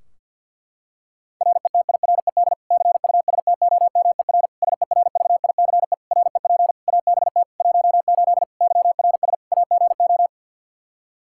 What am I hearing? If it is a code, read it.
Morse code, heard as a keyed tone, about 46 wpm, 700 Hz. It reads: RETIRED CUSTOMER SERVICE REP ABT 18 YRS AGO
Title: Practice QSO - Exchange 3 - More Info - 50wpm